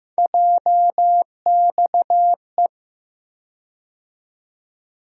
Morse code, heard as a keyed tone, about 16 wpm, 700 Hz. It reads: JXE
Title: USA Call Sign Postfixes - 3 Letters 15wpm